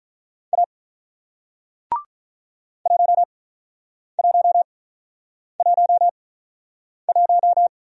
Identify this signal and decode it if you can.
Morse code, heard as a keyed tone, about 46 wpm, spaced as for 9 wpm, 700 Hz. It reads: A 1111